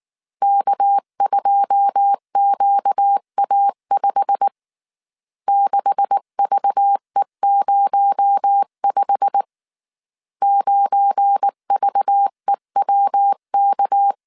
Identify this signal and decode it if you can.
Morse code, heard as a keyed tone, about 19 wpm, 800 Hz. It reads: K2QA5 6VE05 9VEWK
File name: CW / DET / 1 9 1500C S B 18wpm 2551